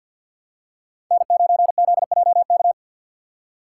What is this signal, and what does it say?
Morse code, heard as a keyed tone, about 52 wpm, 700 Hz. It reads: N9ZJK
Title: USA Call Signs - 5 Characters - 6x Character Spacing - 50wpm